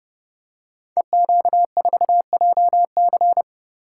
Morse code, heard as a keyed tone, about 30 wpm, 700 Hz. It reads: EQ4JC
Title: Code Groups - 5 Letter:Numbers - 30wpm